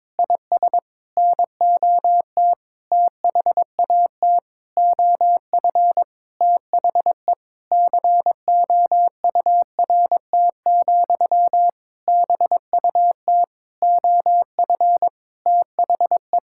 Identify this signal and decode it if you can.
Morse code, heard as a keyed tone, about 22 wpm, 700 Hz. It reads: IS NOT THAT OF THE COURT, BUT OF THE
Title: Sentences from Top 1600 Words 22wpm